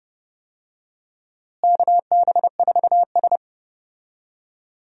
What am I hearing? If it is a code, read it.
Morse code, heard as a keyed tone, about 32 wpm, 700 Hz. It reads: KB4S
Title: USA Call Signs - 4 Characters - Repeated 2x - 30wpm